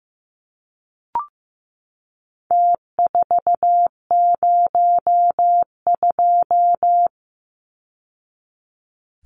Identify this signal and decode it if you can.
Morse code, heard as a keyed tone, about 15 wpm, 700 Hz. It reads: T402
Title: Numbers - Contest Style - 4-Digit 15wpm